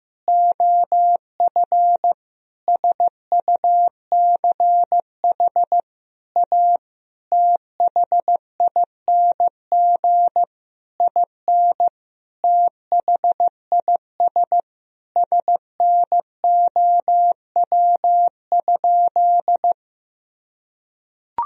Morse code, heard as a keyed tone, about 15 wpm, 700 Hz. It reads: OF SUCH A THING IN THIS SNOW?
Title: Sentences from Top 1000 Words - Review Part 3 - 15wpm